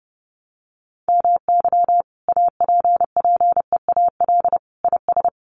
Morse code, heard as a keyed tone, about 30 wpm, 700 Hz. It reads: MY APPEAL IS
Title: Sentences from Top 1300 Words 30wpm